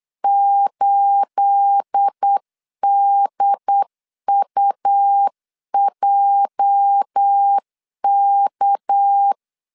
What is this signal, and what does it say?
Morse code, heard as a keyed tone, about 8 wpm, 800 Hz. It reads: 8DUJK
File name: LEZIONI / 9 28 8wpm 3512